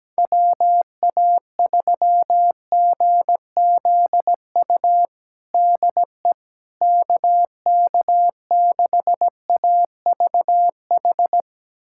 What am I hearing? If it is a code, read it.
Morse code, heard as a keyed tone, about 17 wpm, 700 Hz. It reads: WA3GZU DE KK6AVH
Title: USA Callsign DE Callsign - 6 Characters - 17wpm